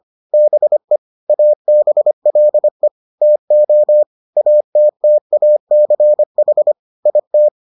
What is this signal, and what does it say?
Morse code, heard as a keyed tone, about 25 wpm, 600 Hz. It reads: BE ABLE TO ATTACH IT